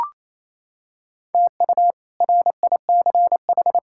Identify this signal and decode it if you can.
Morse code, heard as a keyed tone, about 28 wpm, 700 Hz. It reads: TU RICH
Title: K1USN SST QSO Practice - 28wpm